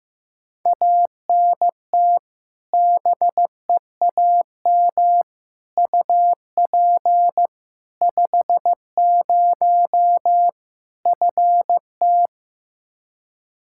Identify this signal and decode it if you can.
Morse code, heard as a keyed tone, about 15 wpm, 700 Hz. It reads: ANT BEAM UP 50 FT
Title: Practice QSO - Exchange 2 - Info - 15wpm